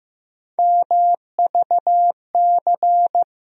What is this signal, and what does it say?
Morse code, heard as a keyed tone, about 15 wpm, 700 Hz. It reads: MVC
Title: USA Call Sign Postfixes - 3 Letters 15wpm